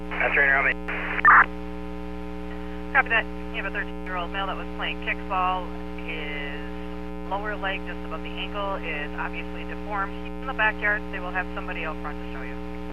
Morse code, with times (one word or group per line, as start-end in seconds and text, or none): none